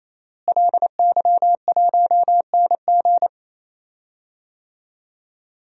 Morse code, heard as a keyed tone, about 28 wpm, 700 Hz. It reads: LY1NG
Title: DXCC-100 Call Signs - 5 Characters - Repeated 3x - 28wpm